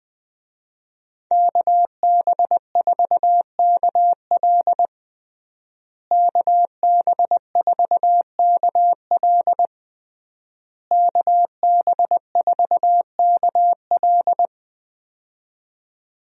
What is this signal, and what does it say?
Morse code, heard as a keyed tone, about 20 wpm, 700 Hz. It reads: KB4KL KB4KL KB4KL
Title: USA Call Signs - 5 Characters - Repeated 3x - 20wpm